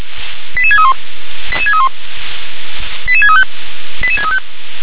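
Five-tone selective call (ZVEI), heard as quick runs of five tones, tones of about 70 ms. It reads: 88631 88631 88635 88635
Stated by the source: ZVEI five-tone selective call received on SDR.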